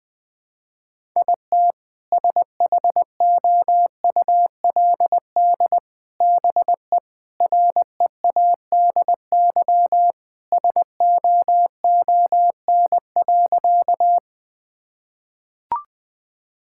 Morse code, heard as a keyed tone, about 21 wpm, 700 Hz. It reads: IT SHOULD BE READY SOON.
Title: Sentences with 2-Word Phrases - 20wpm